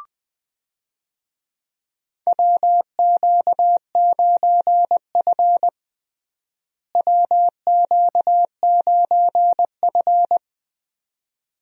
Morse code, heard as a keyed tone, about 21 wpm, 700 Hz. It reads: WQ9F WQ9F
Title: USA Call Signs - 4 Characters - Repeated 2x - 20wpm